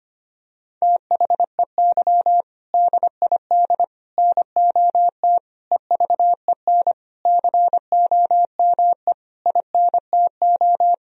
Morse code, heard as a keyed tone, about 25 wpm, 700 Hz. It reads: THEY DID NOT EVEN COME INTO
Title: Sentences from Top 1000 Words - Review Part 1 - 25wpm